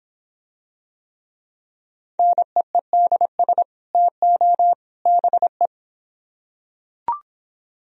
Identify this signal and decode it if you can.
Morse code, heard as a keyed tone, about 24 wpm, 700 Hz. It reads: NEEDS TO BE